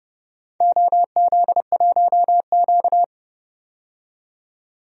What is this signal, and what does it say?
Morse code, heard as a keyed tone, about 30 wpm, 700 Hz. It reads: OZ1Q
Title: DXCC-100 Call Signs - 4 Characters - 30wpm